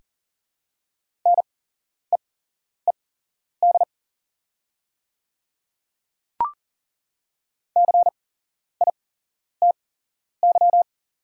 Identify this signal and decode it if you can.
Morse code, heard as a keyed tone, about 41 wpm, spaced as for 11 wpm, 700 Hz. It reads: NEED CITY
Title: 8x Character Spacing - Top 300 Words - 40wpm